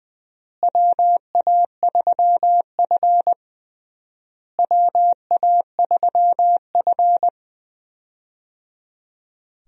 Morse code, heard as a keyed tone, about 21 wpm, 700 Hz. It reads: WA3F WA3F
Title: USA Call Signs - 4 Characters - Repeated 3x - 20wpm